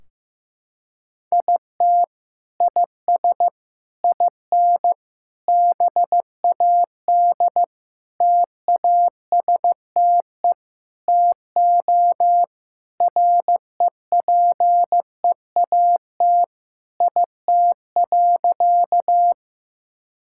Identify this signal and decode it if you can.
Morse code, heard as a keyed tone, about 16 wpm, 700 Hz. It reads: IT IS IN BAD TASTE TO REPEAT IT.
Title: Sentences from Top 1600 Words 15wpm